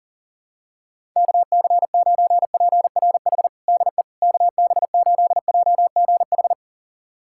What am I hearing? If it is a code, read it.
Morse code, heard as a keyed tone, about 37 wpm, 700 Hz. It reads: KC9PRH DE KB8JGH
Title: USA Callsign DE Callsign - 6 Characters - 40wpm